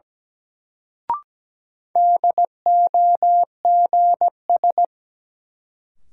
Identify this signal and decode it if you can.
Morse code, heard as a keyed tone, about 17 wpm, 700 Hz. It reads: DOGS
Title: Sentences from Top 1600 Words - New Word Drill - 17wpm